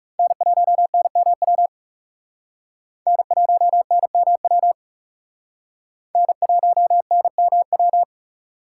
Morse code, heard as a keyed tone, about 37 wpm, 700 Hz. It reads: N1NMW N1NMW N1NMW
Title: Speed Racing - USA 1x3 Call Signs 30wpm